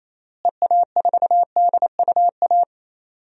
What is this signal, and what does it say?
Morse code, heard as a keyed tone, about 28 wpm, 700 Hz. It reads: EA4DUA
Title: DXCC-100 Call Signs - 6 Characters - Repeated 3x - 28wpm